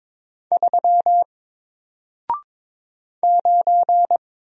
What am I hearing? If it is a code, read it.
Morse code, heard as a keyed tone, about 22 wpm, 700 Hz. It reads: 3 9
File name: Single-Letter-Number-22wpm